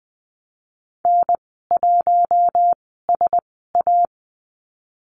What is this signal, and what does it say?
Morse code, heard as a keyed tone, about 20 wpm, spaced as for 14 wpm, 700 Hz. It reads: N1SA
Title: 055 Lesson 10g - Letter r - Call Signs 20wpm